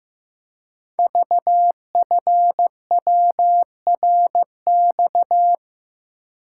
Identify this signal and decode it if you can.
Morse code, heard as a keyed tone, about 15 wpm, 700 Hz. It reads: VFWRX
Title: Code Groups - 5 Letters - 15wpm